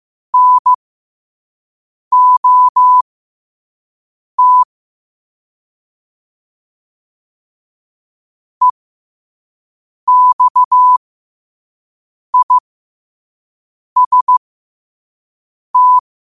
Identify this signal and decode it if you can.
Morse code, heard as a keyed tone, about 15 wpm, spaced as for 5 wpm, 1000 Hz. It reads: NOT EXIST
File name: qotdfiles / 5wpm